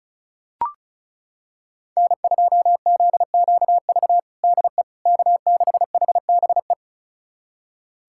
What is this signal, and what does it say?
Morse code, heard as a keyed tone, about 34 wpm, 700 Hz. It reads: N2ZQV DE K6HBE
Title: USA Callsign DE Callsign 35wpm